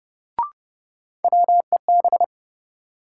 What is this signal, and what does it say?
Morse code, heard as a keyed tone, about 30 wpm, 700 Hz. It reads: WEB
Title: Sentences from Top 1000 Words - Review Part 4 - New Word Drill - 30wpm